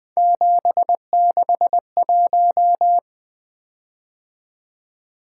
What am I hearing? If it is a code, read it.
Morse code, heard as a keyed tone, about 20 wpm, 700 Hz. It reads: 761
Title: Three Letter-Number Combinations - 20wpm